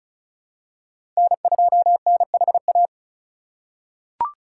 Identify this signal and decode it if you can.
Morse code, heard as a keyed tone, about 35 wpm, 690 Hz. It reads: N2NHA
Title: USA Call Signs - 5 Characters - 2x Character Spacing - 35wpm